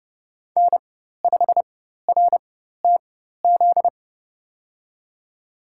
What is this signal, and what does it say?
Morse code, heard as a keyed tone, about 30 wpm, spaced as for 14 wpm, 700 Hz. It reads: N5RTZ
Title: USA Call Signs - 5 Characters - 4x Character Spacing - 30wpm